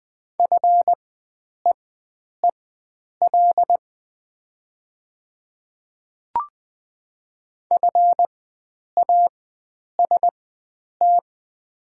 Morse code, heard as a keyed tone, about 20 wpm, spaced as for 9 wpm, 700 Hz. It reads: FEEL FAST
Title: 4x Character Spacing - Top 500 Words - 20wpm